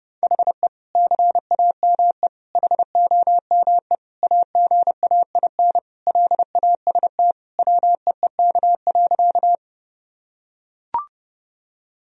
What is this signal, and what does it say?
Morse code, heard as a keyed tone, about 30 wpm, 700 Hz. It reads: HE CAME HOME AGAIN LAST WEEK.
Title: Sentences from Top 1000 Words - Review Part 2 - 30wpm